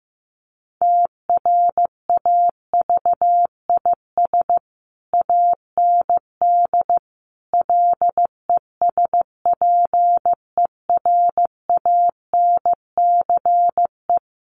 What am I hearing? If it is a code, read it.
Morse code, heard as a keyed tone, about 15 wpm, 700 Hz. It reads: TRAVIS AND LESPERANCE
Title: A Sound Of Thunder 15wpm